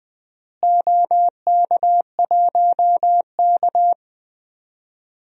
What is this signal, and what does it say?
Morse code, heard as a keyed tone, about 21 wpm, 700 Hz. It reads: OK1K
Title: DXCC-100 Call Signs - 4 Characters - Repeated 3x - 20wpm